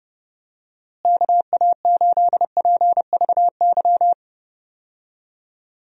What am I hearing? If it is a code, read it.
Morse code, heard as a keyed tone, about 30 wpm, 700 Hz. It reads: KA8PVY
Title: USA Call Signs - 6 Characters - Repeated 3x - 30wpm